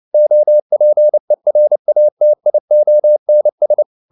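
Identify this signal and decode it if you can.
Morse code, heard as a keyed tone, about 29 wpm, 600 Hz. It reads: OPERATIONS
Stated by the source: Morse code at 29 wpm, sending Ken Follett's novel Never in English, from CW med Gnister.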